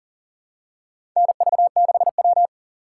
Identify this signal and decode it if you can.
Morse code, heard as a keyed tone, about 40 wpm, 700 Hz. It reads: NV6W
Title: USA Call Signs - 4 Characters - 3x Character Spacing - 40wpm